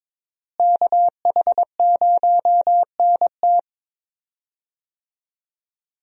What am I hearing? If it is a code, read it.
Morse code, heard as a keyed tone, about 22 wpm, 700 Hz. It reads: KH0NT
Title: USA Call Signs - 5 Characters - Repeated 2x - 22wpm